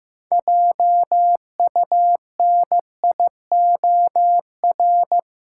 Morse code, heard as a keyed tone, about 15 wpm, 700 Hz. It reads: JUNIOR
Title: Sentences from Top 1000 Words - Review Part 7 - New Word Drill - 15wpm